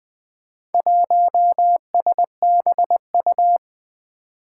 Morse code, heard as a keyed tone, about 21 wpm, 700 Hz. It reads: 1SBU
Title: Four Challenging Letter-Number Combinations 20wpm